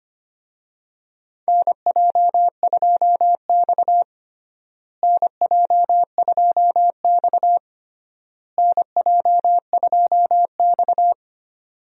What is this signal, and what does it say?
Morse code, heard as a keyed tone, about 25 wpm, 700 Hz. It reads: NJ2X NJ2X NJ2X
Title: USA Call Signs - 4 Characters - Repeated 3x - 25wpm